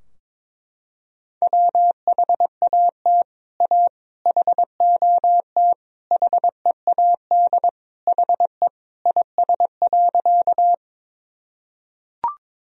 Morse code, heard as a keyed tone, about 22 wpm, 700 Hz. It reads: WHAT A HOT HEAD HE IS.
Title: Sentences from Top 1000 Words - Review Part 3 - 22wpm